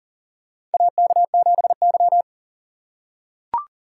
Morse code, heard as a keyed tone, about 39 wpm, 700 Hz. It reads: AK7Y
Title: USA Call Signs - 3x Character Spacing - 40wpm